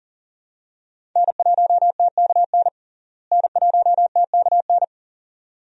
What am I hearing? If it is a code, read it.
Morse code, heard as a keyed tone, about 40 wpm, 700 Hz. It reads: N1TKN N1TKN
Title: USA Call Signs - 5 Characters - Repeated 2x - 40wpm